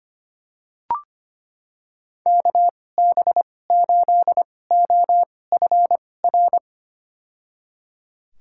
Morse code, heard as a keyed tone, about 25 wpm, spaced as for 18 wpm, 700 Hz. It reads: KB8OFR
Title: USA Call Signs - 6 Characters - 2x Character Spacing - 25wpm